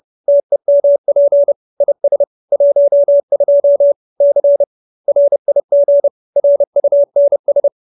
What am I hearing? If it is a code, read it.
Morse code, heard as a keyed tone, about 30 wpm, 570 Hz. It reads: TEMP IS 12 C RIG RUNS